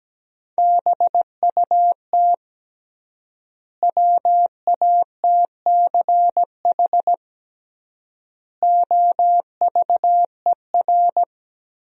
Morse code, heard as a keyed tone, about 17 wpm, 700 Hz. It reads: BUT WATCH OVER
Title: 3x Word-Spacing - Sentences from Top 300 Words - 17wpm